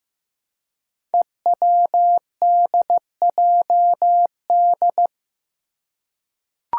Morse code, heard as a keyed tone, about 15 wpm, 700 Hz. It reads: EWDJD
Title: Five Challenging Letter-Number Combinations 15wpm